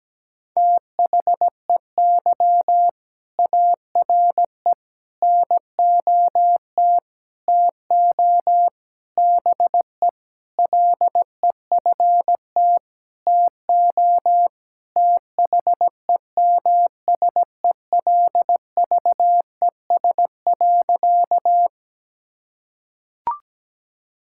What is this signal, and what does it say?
Morse code, heard as a keyed tone, about 17 wpm, 700 Hz. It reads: THEY ARE NOT TO BE LEFT TO THEMSELVES.
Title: Sentences from Top 1000 Words - Review Part 5 - 17wpm